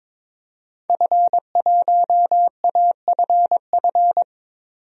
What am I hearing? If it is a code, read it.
Morse code, heard as a keyed tone, about 21 wpm, 700 Hz. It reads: F1AFF
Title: DXCC-100 Call Signs - 5 Characters - 22wpm